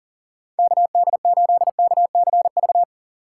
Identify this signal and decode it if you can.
Morse code, heard as a keyed tone, about 40 wpm, 700 Hz. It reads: KD8KCV
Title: USA Call Signs - 6 Characters - 3x Character Spacing - 40wpm